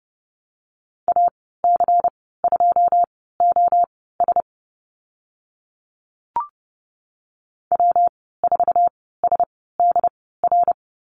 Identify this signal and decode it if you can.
Morse code, heard as a keyed tone, about 30 wpm, spaced as for 17 wpm, 700 Hz. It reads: AC2OS W4SDR
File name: Lesson 19g - Letter w - Call Signs 30wpm